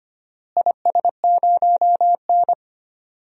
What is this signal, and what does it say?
Morse code, heard as a keyed tone, about 25 wpm, 700 Hz. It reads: IS0N
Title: DXCC-100 Call Signs - 4 Characters - Repeated 2x - 25wpm